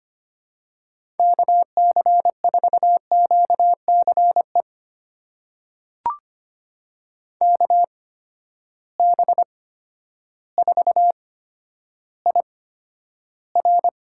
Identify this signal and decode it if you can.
Morse code, heard as a keyed tone, about 25 wpm, spaced as for 11 wpm, 700 Hz. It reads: KC4QCE KB4IR
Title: USA Call Signs - 6 Characters - 8x Character Spacing - 25wpm